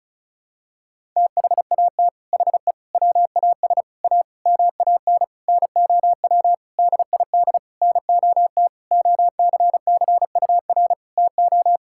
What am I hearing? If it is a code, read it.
Morse code, heard as a keyed tone, about 36 wpm, 690 Hz. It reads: THAT HE WAS A MAN NOW DID NOT OCCUR TO
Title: Sentences from Top 1000 Words - Review Part 6 - 35wpm